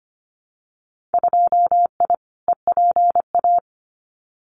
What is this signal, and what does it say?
Morse code, heard as a keyed tone, about 25 wpm, 700 Hz. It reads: 2I EPA
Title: Winter Field Day Exchanges 25wpm